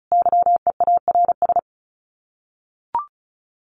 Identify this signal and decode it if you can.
Morse code, heard as a keyed tone, about 34 wpm, 700 Hz. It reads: YEARS
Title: Sets of 5 Words 35wpm